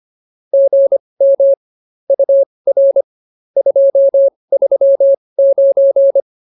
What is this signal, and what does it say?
Morse code, heard as a keyed tone, about 24 wpm, spaced as for 20 wpm, 550 Hz. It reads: GM UR 239